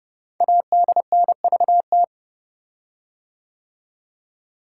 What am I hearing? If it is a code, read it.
Morse code, heard as a keyed tone, about 30 wpm, 700 Hz. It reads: ADNVT
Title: Code Groups - 5 Letters - 30wpm